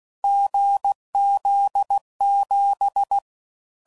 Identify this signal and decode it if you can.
Morse code, heard as a keyed tone, about 16 wpm, 800 Hz. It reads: GZ7
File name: Combinations / Difficult Combos / GZ GZ7